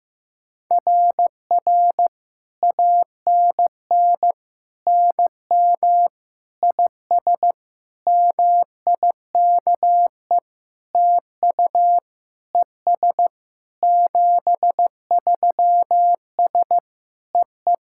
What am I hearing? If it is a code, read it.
Morse code, heard as a keyed tone, about 15 wpm, 700 Hz. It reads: RR ANN NM IS MIKE TU ES 73S EE
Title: SOTA QSO Practice 15wpm